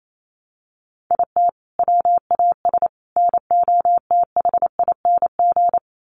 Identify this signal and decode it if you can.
Morse code, heard as a keyed tone, about 28 wpm, 700 Hz. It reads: IT WAS NOTHING
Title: Sentences from Top 1300 Words 28wpm